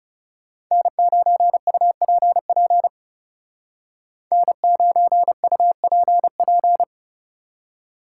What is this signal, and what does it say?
Morse code, heard as a keyed tone, about 37 wpm, 700 Hz. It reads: N9UPP N9UPP
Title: Speed Racing - USA Call Signs 30wpm